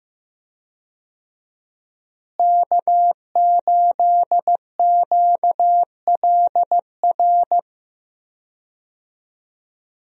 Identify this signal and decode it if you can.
Morse code, heard as a keyed tone, about 15 wpm, 700 Hz. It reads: K8QLR
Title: USA Call Signs - 5 Characters - Repeated 3x - 15wpm